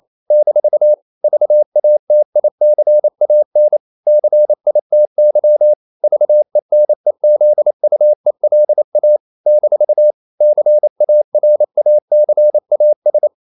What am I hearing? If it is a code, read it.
Morse code, heard as a keyed tone, about 28 wpm, 600 Hz. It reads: - VATICAN CITY VENEZUELA - CARACAS
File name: CW / Practice / NationalCapitals28